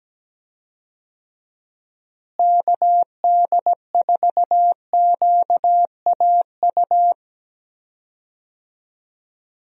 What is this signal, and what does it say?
Morse code, heard as a keyed tone, about 17 wpm, 700 Hz. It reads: KD4QAU
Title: USA Call Signs - 6 Characters - Repeated 3x - 17wpm